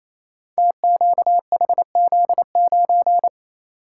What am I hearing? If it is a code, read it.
Morse code, heard as a keyed tone, about 28 wpm, 700 Hz. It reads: TQHZ9
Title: Code Groups - 5 Letter:Numbers - 28wpm